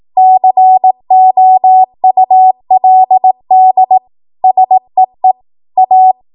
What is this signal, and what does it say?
Morse code, heard as a keyed tone, about 18 wpm, 750 Hz. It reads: COULD SEE A
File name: Archive / 18 WPM / 18WPM